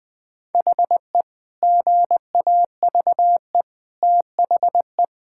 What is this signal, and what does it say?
Morse code, heard as a keyed tone, about 20 wpm, 700 Hz. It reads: HE GAVE THE
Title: Sentences with 4-Word Phrases - 20wpm